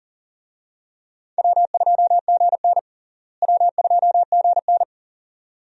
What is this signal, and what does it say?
Morse code, heard as a keyed tone, about 37 wpm, 700 Hz. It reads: W2GN W2GN
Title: USA Call Signs - 4 Characters - Repeated 2x - 40wpm